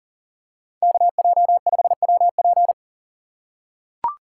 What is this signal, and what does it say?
Morse code, heard as a keyed tone, about 40 wpm, 700 Hz. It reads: KJ5WP
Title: USA Call Signs - 5 Characters - 3x Character Spacing - 40wpm